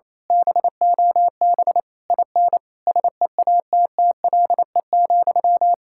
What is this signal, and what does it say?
Morse code, heard as a keyed tone, about 27 wpm, 700 Hz. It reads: BOB IN SEATTLE,
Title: Speed-Racing Common QSO Phrases - 28wpm